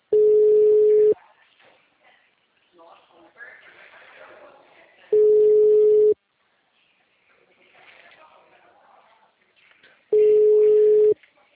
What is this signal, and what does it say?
Ringback tone (Europe, 425 Hz), with speech.